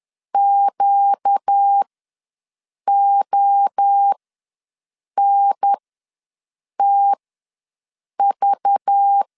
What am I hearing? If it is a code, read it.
Morse code, heard as a keyed tone, about 10 wpm, spaced as for 6 wpm, 800 Hz. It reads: QONTV
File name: LEZIONI / 6 PKX0 / 6 PKX ws 10wpm 3706